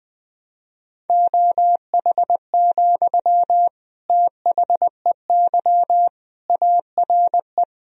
Morse code, heard as a keyed tone, about 20 wpm, 700 Hz. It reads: OH, THEY ARE
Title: Sentences from Top 1000 Words - Review Part 4 - 20wpm